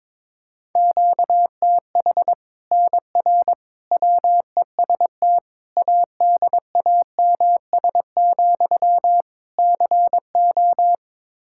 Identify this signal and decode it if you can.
Morse code, heard as a keyed tone, about 22 wpm, 700 Hz. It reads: QTH NR WEST ADAMS, CO
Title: Practice QSO - Exchange 1 - Beginning - 22wpm